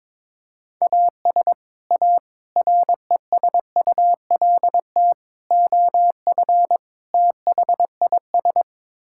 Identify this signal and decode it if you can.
Morse code, heard as a keyed tone, about 22 wpm, 700 Hz. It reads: AS A RESULT OF THIS
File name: practice / 22wpm / Build A Phrase - 5-Word Phrases - 22wpm